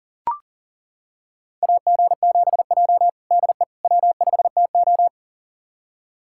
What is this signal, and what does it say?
Morse code, heard as a keyed tone, about 40 wpm, 700 Hz. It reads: AG7J DE W5TO
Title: USA Callsign DE Callsign - 4 Characters - 40wpm